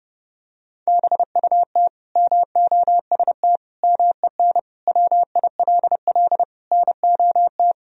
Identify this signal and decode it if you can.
Morse code, heard as a keyed tone, about 32 wpm, 700 Hz. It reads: BUT MOST MEN WILL NOT
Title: Sentences from Top 1000 Words - Review Part 1 - 30wpm